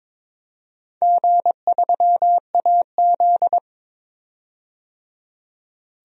Morse code, heard as a keyed tone, about 22 wpm, 700 Hz. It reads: G3AZ
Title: DXCC-100 Call Signs - 4 Characters - 22wpm